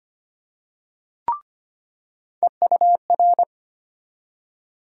Morse code, heard as a keyed tone, about 25 wpm, 700 Hz. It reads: EUR
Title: USA Call Sign Postfixes - 3 Letters 25wpm